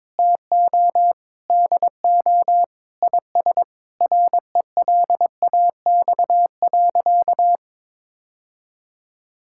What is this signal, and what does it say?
Morse code, heard as a keyed tone, about 22 wpm, 700 Hz. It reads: TO DO IS RELAX.
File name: practice / 22wpm / Sentences with 8-Word Phrases - 22wpm